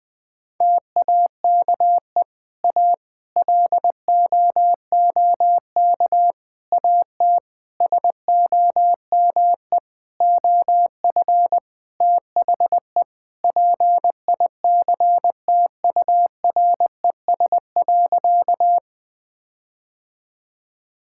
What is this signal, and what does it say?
Morse code, heard as a keyed tone, about 21 wpm, 700 Hz. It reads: TAKE A LOOK AT SOME OF THE PICTURES.
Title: Sentences with 7-Word Phrases - Spoken Beginning - 20wpm